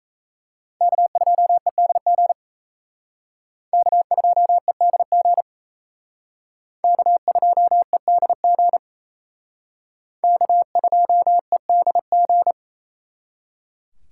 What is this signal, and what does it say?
Morse code, heard as a keyed tone, about 35 wpm, 700 Hz. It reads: K2EDG K2EDG K2EDG K2EDG